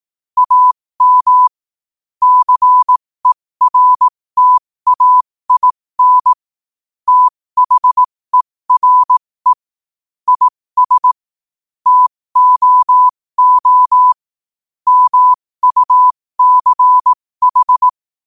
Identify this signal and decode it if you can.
Morse code, heard as a keyed tone, about 18 wpm, spaced as for 15 wpm, 1000 Hz. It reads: AM CERTAIN THERE IS TOO MUCH
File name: qotdfiles / 15wpm